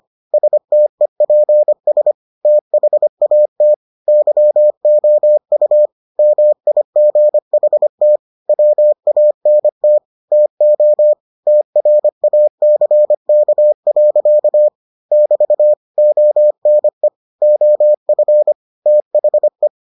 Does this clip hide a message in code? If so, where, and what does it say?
Yes: Morse code from 0.3 to 19.7 s, heard as a keyed tone, about 25 wpm, 600 Hz. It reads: STEPS THAT YOU MIGHT WANT TO TRACK. <BT> ONE OF THE